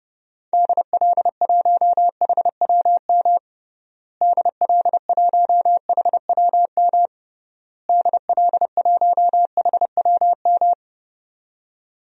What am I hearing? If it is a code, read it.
Morse code, heard as a keyed tone, about 32 wpm, 700 Hz. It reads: DL1HWM DL1HWM DL1HWM
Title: DXCC-100 Call Signs - 6 Characters - Repeated 3x - 30wpm